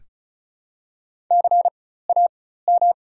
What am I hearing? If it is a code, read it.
Morse code, heard as a keyed tone, about 36 wpm, spaced as for 16 wpm, 700 Hz. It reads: CAM
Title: 4x Character Spacing - Top 200 Words - 35wpm